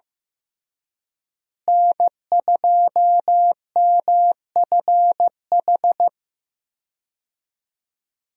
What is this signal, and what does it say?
Morse code, heard as a keyed tone, about 15 wpm, 700 Hz. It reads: N2MFH
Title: USA Call Signs - 5 Characters - Repeated 2x - 15wpm